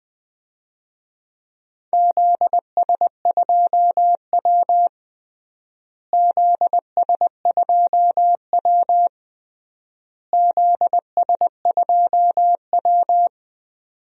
Morse code, heard as a keyed tone, about 20 wpm, 700 Hz. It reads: ZS2W ZS2W ZS2W
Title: DXCC-100 Call Signs - 4 Characters - Repeated 3x - 20wpm